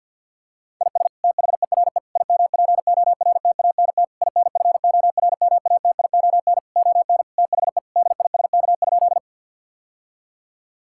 Morse code, heard as a keyed tone, about 50 wpm, 700 Hz. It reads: IS THERE IMPORTANT INFORMATION ON THE DISK?
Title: Sentences from Top 1000 Words - Review Part 6 - 50wpm